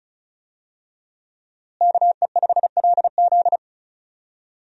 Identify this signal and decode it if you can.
Morse code, heard as a keyed tone, about 36 wpm, 700 Hz. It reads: KE5LZ
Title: USA Call Signs - 5 Characters - Repeated 3x - 35wpm